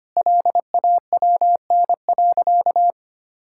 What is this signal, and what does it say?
Morse code, heard as a keyed tone, about 25 wpm, 700 Hz. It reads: LAWN.
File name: Easy Sentences with Hints - 25wpm